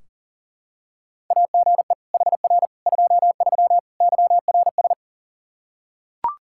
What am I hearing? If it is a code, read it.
Morse code, heard as a keyed tone, about 40 wpm, 700 Hz. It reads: AGE HR 23 YRS
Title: Practice QSO - Exchange 3 - More Info - 40wpm